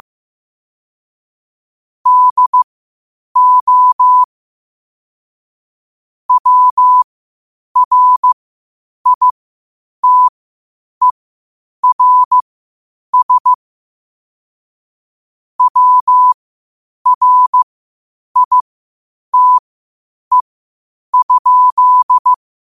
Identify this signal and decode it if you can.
Morse code, heard as a keyed tone, about 15 wpm, spaced as for 8 wpm, 1000 Hz. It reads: DO WRITERS WRITE?
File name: qotdfiles / 8wpm